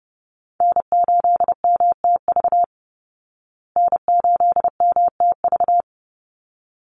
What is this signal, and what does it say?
Morse code, heard as a keyed tone, about 30 wpm, 700 Hz. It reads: N8MTV N8MTV
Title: Call Signs Repeated 3x - 30wpm